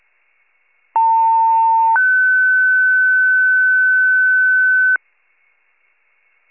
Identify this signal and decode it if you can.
Two-tone page, heard as a one-second tone, then a three-second tone, about 900 then 1500 Hz. It is heard from 1.0 to 5.0 s.
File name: wcalert